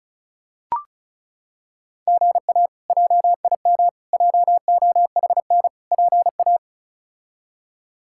Morse code, heard as a keyed tone, about 34 wpm, 700 Hz. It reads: GA JIM JOHN PA
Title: K1USN SST QSO Practice - 35wpm